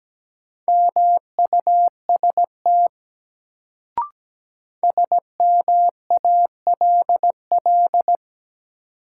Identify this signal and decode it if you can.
Morse code, heard as a keyed tone, about 17 wpm, 700 Hz. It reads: MUST SMALL
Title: Sets of 2 Words - New Word Drill - 17wpm